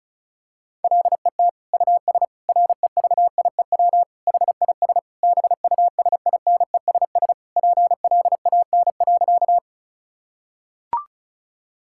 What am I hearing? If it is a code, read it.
Morse code, heard as a keyed tone, about 36 wpm, 700 Hz. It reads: LET US REVIEW HIS BUSINESS PLAN.
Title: Sentences from Top 1000 Words - Review Part 6 - 35wpm